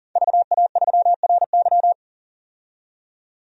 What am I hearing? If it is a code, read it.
Morse code, heard as a keyed tone, about 38 wpm, 700 Hz. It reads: VA3RY
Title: DXCC-100 Call Signs - 5 Characters - 40wpm